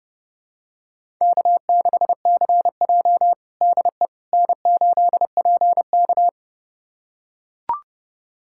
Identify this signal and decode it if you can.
Morse code, heard as a keyed tone, about 30 wpm, 700 Hz. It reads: K6CJ DE N8PK
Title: USA Callsign DE Callsign - 4 Characters - 30wpm